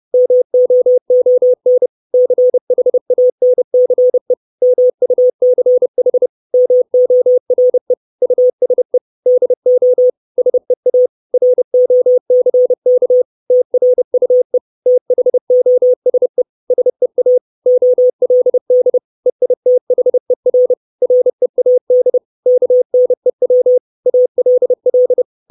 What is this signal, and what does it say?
Morse code, heard as a keyed tone, about 30 wpm, 500 Hz. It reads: MOON CHANCE MUCH MORE USE DO SEA ROCK TRUE THOSE SEA OLD EITHER READ KNEW ALL
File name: 632 Words 30WPM